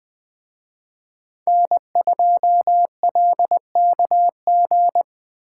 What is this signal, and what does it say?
Morse code, heard as a keyed tone, about 21 wpm, 700 Hz. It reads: N2LKG